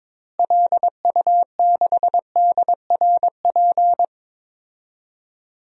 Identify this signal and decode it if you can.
Morse code, heard as a keyed tone, about 22 wpm, 700 Hz. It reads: LU6DRP
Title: DXCC-100 Call Signs - 6 Characters - Repeated 3x - 22wpm